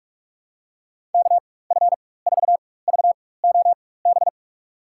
Morse code, heard as a keyed tone, about 45 wpm, spaced as for 21 wpm, 700 Hz. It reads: KF4VOB
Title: USA Call Signs - 4x Character Spacing - 45wpm